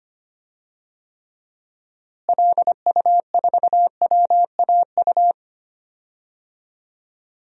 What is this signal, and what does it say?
Morse code, heard as a keyed tone, about 25 wpm, 700 Hz. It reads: LU4WAU
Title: DXCC-100 Call Signs - 6 Characters - Repeated 3x - 25wpm